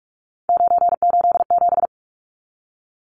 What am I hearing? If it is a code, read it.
Morse code, heard as a keyed tone, about 44 wpm, 700 Hz. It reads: 987
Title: Numbers - 3-Digit 45wpm